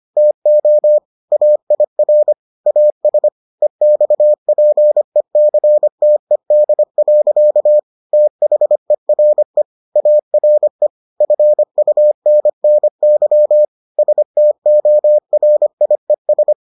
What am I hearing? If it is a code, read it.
Morse code, heard as a keyed tone, about 25 wpm, 600 Hz. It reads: TO AIR AS EXPECTED. THERE ARE FUNNY STORIES